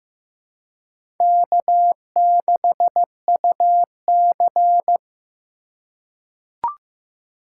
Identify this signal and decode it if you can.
Morse code, heard as a keyed tone, about 15 wpm, 700 Hz. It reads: K6UC
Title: USA Call Signs - 4 Characters - 2x Character Spacing - 15wpm